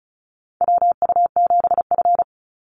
Morse code, heard as a keyed tone, about 34 wpm, 700 Hz. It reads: WU7F
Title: Call Signs - 35wpm